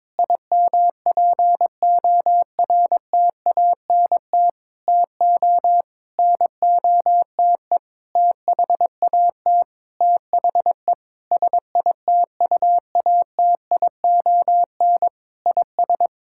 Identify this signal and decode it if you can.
Morse code, heard as a keyed tone, about 22 wpm, 700 Hz. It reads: IMPORTANT TO NOTE THAT THE SITUATION IS
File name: practice / 22wpm / Sentences with 8-Word Phrases - 22wpm